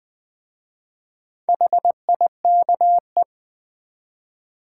Morse code, heard as a keyed tone, about 20 wpm, 700 Hz. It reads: HIKE
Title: Easy Words with Hints - 20wpm